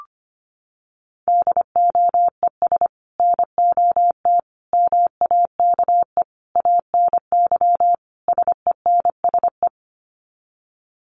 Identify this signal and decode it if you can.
Morse code, heard as a keyed tone, about 25 wpm, 700 Hz. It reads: DOES NOT MAKE ANY SENSE
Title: Sets of 5 Words Encore2 25wpm